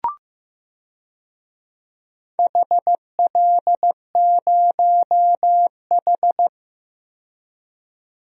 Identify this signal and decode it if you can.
Morse code, heard as a keyed tone, about 15 wpm, 700 Hz. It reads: HL0H
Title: DXCC-100 Call Signs - 4 Characters - 15wpm